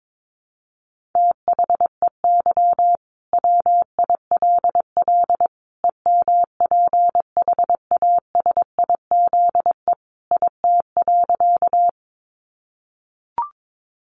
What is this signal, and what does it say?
Morse code, heard as a keyed tone, about 22 wpm, 700 Hz. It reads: THEY WILL EMPHASIZE IT.
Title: Sentences from Top 1500 Words 22wpm